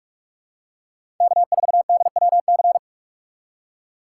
Morse code, heard as a keyed tone, about 46 wpm, 700 Hz. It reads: K4DWC
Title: USA Call Signs - 5 Characters - 2x Character Spacing - 45wpm